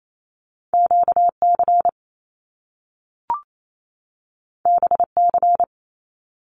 Morse code, heard as a keyed tone, about 28 wpm, 700 Hz. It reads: QC BC
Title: Canadian Provinces and Territories 28wpm